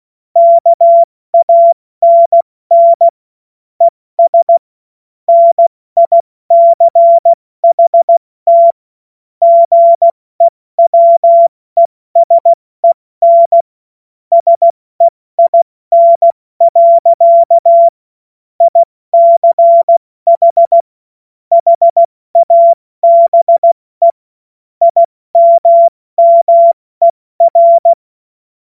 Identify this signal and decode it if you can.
Morse code, heard as a keyed tone, about 16 wpm, 680 Hz. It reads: KANN ES NICHT GEWESEN SEIN. ICH HABE IMMER